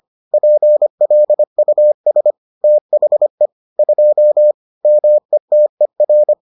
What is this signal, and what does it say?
Morse code, heard as a keyed tone, about 25 wpm, 600 Hz. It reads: PLUS THE 2 METER